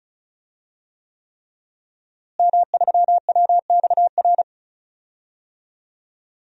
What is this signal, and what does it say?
Morse code, heard as a keyed tone, about 36 wpm, 700 Hz. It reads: M3WXR
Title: DXCC-100 Call Signs - 5 Characters - Repeated 3x - 35wpm